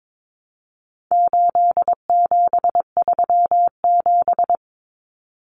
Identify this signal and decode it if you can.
Morse code, heard as a keyed tone, about 22 wpm, 700 Hz. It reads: 8737
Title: Common Cut Numbers - 4-Digit 22wpm